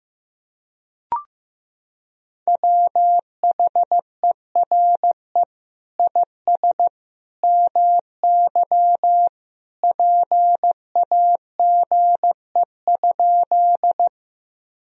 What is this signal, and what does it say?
Morse code, heard as a keyed tone, about 16 wpm, 700 Hz. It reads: WHERE IS MY PAGE?
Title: Sentences from Top 1000 Words - Review Part 1 - 15wpm